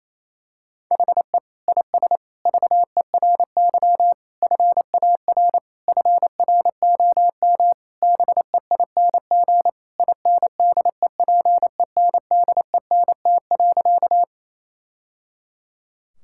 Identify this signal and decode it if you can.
Morse code, heard as a keyed tone, about 28 wpm, 700 Hz. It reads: HE IS VERY FAR FROM BEING INDEPENDENT.
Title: Sentences from Top 1000 Words - Review Part 6 - 28wpm